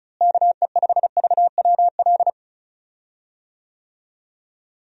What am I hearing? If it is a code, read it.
Morse code, heard as a keyed tone, about 36 wpm, 700 Hz. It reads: KE5VWL
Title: USA Call Signs - 6 Characters - Repeated 2x - 35wpm